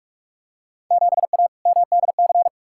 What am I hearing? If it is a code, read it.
Morse code, heard as a keyed tone, about 42 wpm, 700 Hz. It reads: 7A MDC
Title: Field Day Exchanges Encore - 45wpm